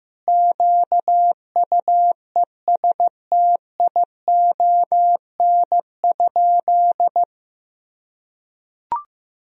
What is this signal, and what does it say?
Morse code, heard as a keyed tone, about 15 wpm, 700 Hz. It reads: QUESTION?
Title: Sentences from Top 400 Words and Abbreviations 15wpm